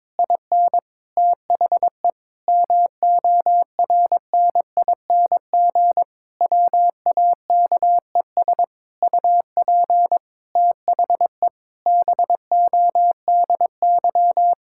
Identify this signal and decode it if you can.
Morse code, heard as a keyed tone, about 22 wpm, 700 Hz. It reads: IN THE MORNING WAKES UP THE BODY
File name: Difficult Sentences with Hints - 22wpm